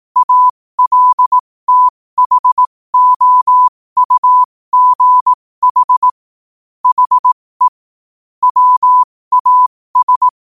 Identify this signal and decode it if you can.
Morse code, heard as a keyed tone, about 18 wpm, spaced as for 15 wpm, 1000 Hz. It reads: ALTHOUGH HE WAS
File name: qotdfiles / 15wpm